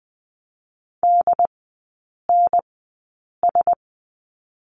Morse code, heard as a keyed tone, about 20 wpm, spaced as for 8 wpm, 700 Hz. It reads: DNS
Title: 067 Lesson 12e - Letter d - Three Characters 20wpm